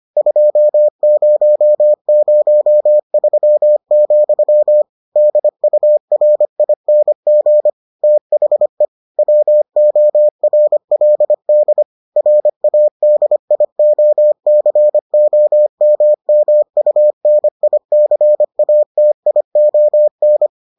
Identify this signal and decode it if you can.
Morse code, heard as a keyed tone, about 25 wpm, 600 Hz. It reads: 2003, DURING THE WORLD RADIOCOMMUNICATION